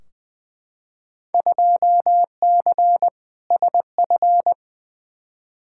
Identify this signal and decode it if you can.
Morse code, heard as a keyed tone, about 20 wpm, 700 Hz. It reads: 2C SF
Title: Field Day Exchanges Encore - 20wpm